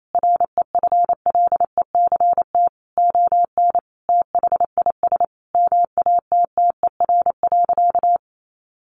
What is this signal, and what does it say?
Morse code, heard as a keyed tone, about 28 wpm, 700 Hz. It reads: REFLECT ON THIS MATTER.
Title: Sentences from Top 1500 Words 28wpm